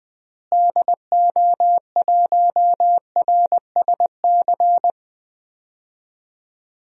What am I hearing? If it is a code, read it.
Morse code, heard as a keyed tone, about 20 wpm, 700 Hz. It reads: DO1RSC
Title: DXCC-100 Call Signs - 6 Characters - Repeated 2x - 20wpm